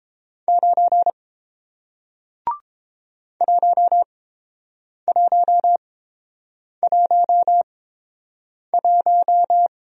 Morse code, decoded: 9 1111

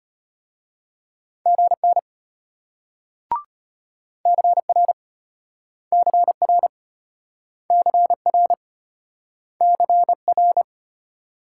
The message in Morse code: GN CR CR CR CR

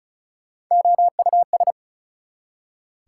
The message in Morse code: OUS

